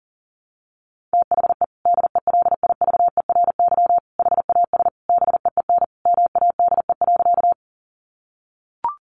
Morse code, heard as a keyed tone, about 40 wpm, 700 Hz. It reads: THE DELIVERY HAS BEEN MADE.
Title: Sentences from Top 1500 Words 40wpm